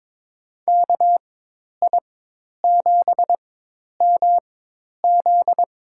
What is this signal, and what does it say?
Morse code, heard as a keyed tone, about 22 wpm, spaced as for 10 wpm, 700 Hz. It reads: KI7MZ
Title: USA Call Signs - 4x Character Spacing - 22wpm